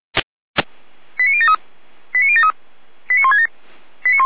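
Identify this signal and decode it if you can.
Five-tone selective call (ZVEI), heard as quick runs of five tones, tones of about 70 ms. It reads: 89073 89073 89167
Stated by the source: Five-tone ZVEI selective call received on an SDR.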